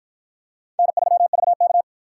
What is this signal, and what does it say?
Morse code, heard as a keyed tone, about 53 wpm, 700 Hz. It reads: N3VK